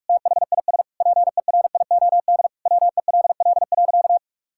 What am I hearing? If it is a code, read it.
Morse code, heard as a keyed tone, about 46 wpm, 700 Hz. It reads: THIS PERIOD WELL.
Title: Sentences from Top 1000 Words - Review Part 5 - 45wpm